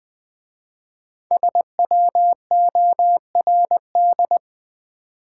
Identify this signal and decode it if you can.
Morse code, heard as a keyed tone, about 21 wpm, 700 Hz. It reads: SWORD
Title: 5 Letter Words - First 3 Letters Given - 20wpm